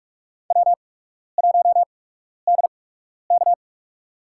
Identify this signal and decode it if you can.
Morse code, heard as a keyed tone, about 46 wpm, spaced as for 12 wpm, 690 Hz. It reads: W1DK
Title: USA Call Signs - 4 Characters - 8x Character Spacing - 45wpm